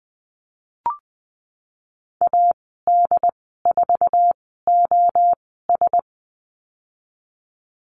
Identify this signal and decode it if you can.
Morse code, heard as a keyed tone, about 20 wpm, spaced as for 14 wpm, 700 Hz. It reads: AD4OS
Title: 082 Lesson 14g - Number 2 - Call Signs 20wpm